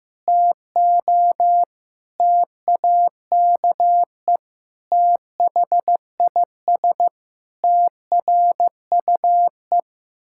Morse code, heard as a keyed tone, about 15 wpm, 700 Hz. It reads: TO TAKE THIS TRUE
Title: Pangram Challenge v2 - 15wpm